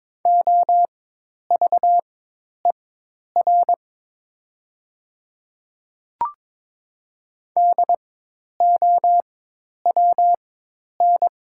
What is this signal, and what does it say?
Morse code, heard as a keyed tone, about 23 wpm, spaced as for 10 wpm, 700 Hz. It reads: OVER DOWN
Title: 4x Character Spacing - Top 300 Words - 22wpm